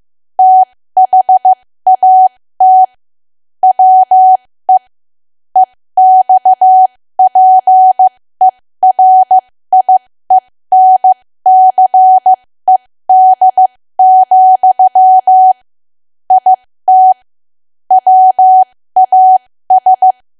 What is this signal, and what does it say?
Morse code, heard as a keyed tone, about 15 wpm, spaced as for 13 wpm, 750 Hz. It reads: THAT WE EXPERIENCED, IT WAS
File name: cwa / W1AW-Level2 / 13WPM